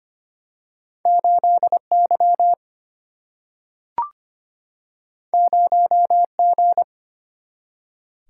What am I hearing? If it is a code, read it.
Morse code, heard as a keyed tone, about 25 wpm, 700 Hz. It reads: 8Y 0G
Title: Single-Letter Single-Number 25wpm